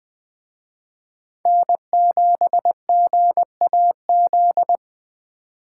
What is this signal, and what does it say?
Morse code, heard as a keyed tone, about 20 wpm, 700 Hz. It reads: N7GAZ